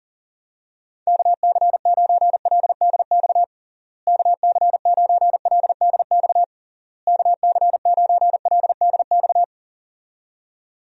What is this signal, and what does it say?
Morse code, heard as a keyed tone, about 40 wpm, 700 Hz. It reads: KC9LDX KC9LDX KC9LDX
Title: USA Call Signs - 6 Characters - Repeated 3x - 40wpm